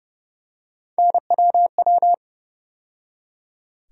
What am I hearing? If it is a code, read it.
Morse code, heard as a keyed tone, about 32 wpm, 700 Hz. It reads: NWW
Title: USA Call Sign Postfixes - 3 Letters 30wpm